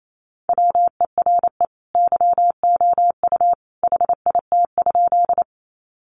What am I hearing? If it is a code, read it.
Morse code, heard as a keyed tone, about 28 wpm, 700 Hz. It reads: WERE YOU HIT?
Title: Sentences from Top 1200 Words 28wpm